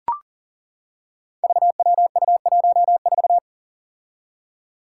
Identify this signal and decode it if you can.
Morse code, heard as a keyed tone, about 38 wpm, 700 Hz. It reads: VWU14
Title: Five Challenging Letter-Number Combinations 40wpm